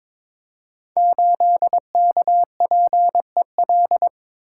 Morse code, heard as a keyed tone, about 22 wpm, 700 Hz. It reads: 8KPEL
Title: Code Groups - 5 Letter:Numbers - 22wpm